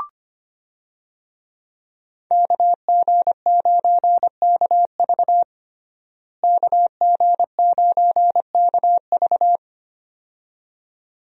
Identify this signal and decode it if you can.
Morse code, heard as a keyed tone, about 25 wpm, 700 Hz. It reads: KG9KV KG9KV
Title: USA Call Signs - 5 Characters - Repeated 2x - 25wpm